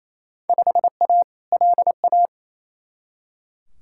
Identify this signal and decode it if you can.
Morse code, heard as a keyed tone, about 28 wpm, 700 Hz. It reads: 5A LA
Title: Field Day Exchanges Encore - 28wpm